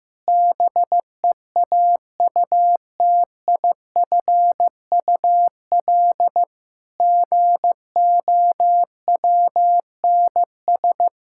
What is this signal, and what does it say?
Morse code, heard as a keyed tone, about 16 wpm, 700 Hz. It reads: BEAUTIFUL GOWNS
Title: Pangram Challenge v2 - 15wpm